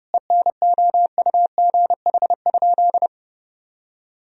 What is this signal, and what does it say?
Morse code, heard as a keyed tone, about 30 wpm, 700 Hz. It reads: ENOUGH?
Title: Sentences from Top 1000 Words - Review Part 3 - 30wpm